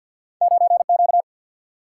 Morse code, heard as a keyed tone, about 50 wpm, 700 Hz. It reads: 9 Q
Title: Single-Letter Single-Number 50wpm